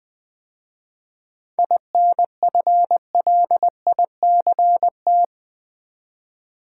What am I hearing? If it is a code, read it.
Morse code, heard as a keyed tone, about 20 wpm, 700 Hz. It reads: INFLICT
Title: 7 Letter Words - First 3 Letters Given - 20wpm